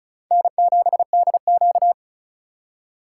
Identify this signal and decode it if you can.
Morse code, heard as a keyed tone, about 34 wpm, 700 Hz. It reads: N7DQ
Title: USA Call Signs - 4 Characters - 4x Character Spacing - 35wpm